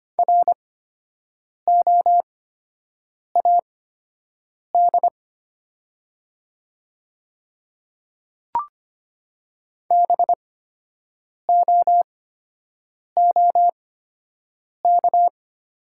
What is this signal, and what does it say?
Morse code, heard as a keyed tone, about 25 wpm, spaced as for 7 wpm, 700 Hz. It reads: ROAD BOOK
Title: 8x Character Spacing - Top 500 Words - 25wpm